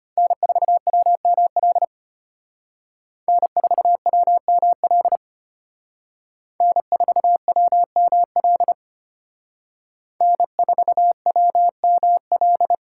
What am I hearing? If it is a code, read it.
Morse code, heard as a keyed tone, about 30 wpm, 700 Hz. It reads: N4WML N4WML N4WML N4WML
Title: Speed Racing - USA 1x3 Call Signs 25wpm